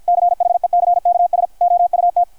Morse code, heard as a keyed tone, about 53 wpm, 660 Hz, with radio noise. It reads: CHECKS OUT
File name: archived / CRSnet